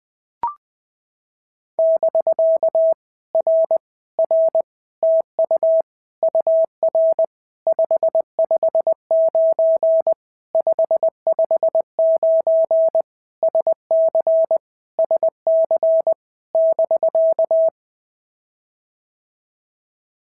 Morse code, heard as a keyed tone, about 20 wpm, 650 Hz. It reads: <BK> R R TU UR 559 559 SC SC <BK>
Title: POTA QSO Practice - 20wpm